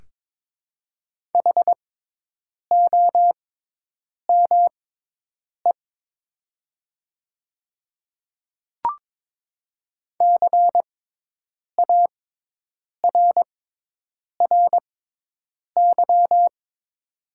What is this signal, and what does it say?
Morse code, heard as a keyed tone, about 22 wpm, spaced as for 8 wpm, 700 Hz. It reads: HOME CARRY
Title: 6x Character Spacing - Top 500 Words - 22wpm